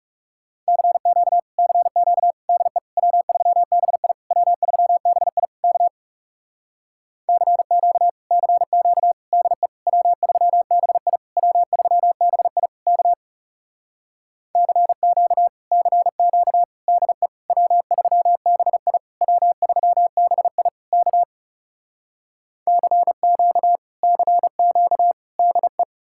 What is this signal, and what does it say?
Morse code, heard as a keyed tone, about 40 wpm, 700 Hz. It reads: CQ CQ DE W3BI W3BI K CQ CQ DE W3BI W3BI K CQ CQ DE W3BI W3BI K CQ CQ DE